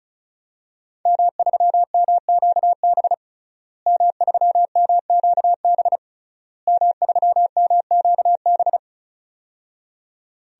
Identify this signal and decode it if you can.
Morse code, heard as a keyed tone, about 36 wpm, 700 Hz. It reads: M3MQB M3MQB M3MQB
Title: DXCC-100 Call Signs - 5 Characters - Repeated 3x - 35wpm